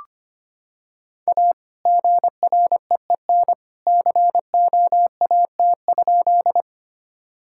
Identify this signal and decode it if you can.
Morse code, heard as a keyed tone, about 25 wpm, 700 Hz. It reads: A GREEN COAT?
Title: Sentences from Top 1600 Words 25wpm